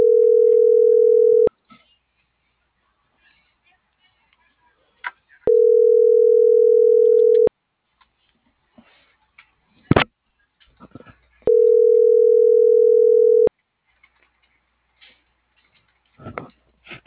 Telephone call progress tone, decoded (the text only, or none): ringback tone